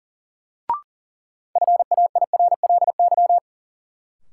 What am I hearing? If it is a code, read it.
Morse code, heard as a keyed tone, about 40 wpm, 700 Hz. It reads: FAIRLY
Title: Sentences from Top 1100 Words - New Word Drill - 40wpm